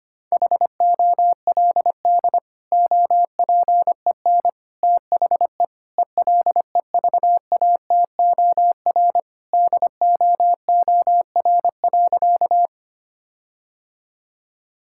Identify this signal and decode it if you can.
Morse code, heard as a keyed tone, about 25 wpm, 700 Hz. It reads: HOLD OPEN THE ELEVATOR DOOR.
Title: Sentences from Top 1600 Words 25wpm